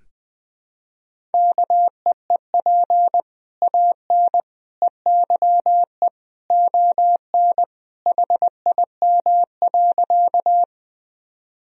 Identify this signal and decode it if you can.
Morse code, heard as a keyed tone, about 21 wpm, 700 Hz. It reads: KEEP AN EYE ON HIM.
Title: Sentences from Top 1000 Words - Review Part 1 - 20wpm